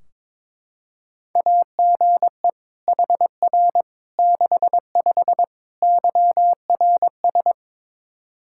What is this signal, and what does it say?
Morse code, heard as a keyed tone, about 22 wpm, 700 Hz. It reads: AGE HR 65 YRS
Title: Practice QSO - Exchange 3 - More Info - 22wpm